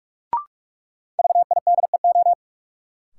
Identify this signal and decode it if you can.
Morse code, heard as a keyed tone, about 46 wpm, 700 Hz. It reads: VIDEO
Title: Sentences from Top 1000 Words - Review Part 4 - New Word Drill - 45wpm